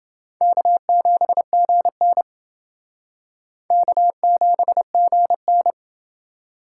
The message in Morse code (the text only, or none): K7GN K7GN